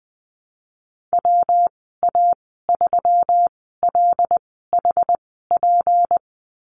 Morse code, heard as a keyed tone, about 20 wpm, spaced as for 15 wpm, 700 Hz. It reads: WA3LHP
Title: 231 Lesson 37g - Letter x - Call Signs 20wpm